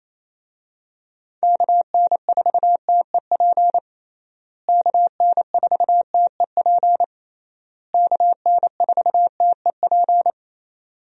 Morse code, heard as a keyed tone, about 28 wpm, 700 Hz. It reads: KN4TEP KN4TEP KN4TEP
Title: USA Call Signs - 6 Characters - Repeated 3x - 28wpm